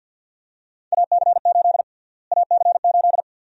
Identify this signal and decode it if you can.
Morse code, heard as a keyed tone, about 51 wpm, 700 Hz. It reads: AC8 AC8
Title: USA Call Sign Prefixes - Repeated 2x - 50wpm